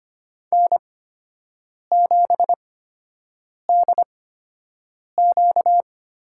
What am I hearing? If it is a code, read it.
Morse code, heard as a keyed tone, about 25 wpm, spaced as for 7 wpm, 700 Hz. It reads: N7DQ
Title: USA Call Signs - 8x Character Spacing - 25wpm